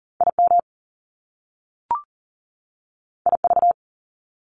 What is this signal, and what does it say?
Morse code, heard as a keyed tone, about 41 wpm, 700 Hz. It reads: IM IV
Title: Top Two Letter Combinations in Words 40wpm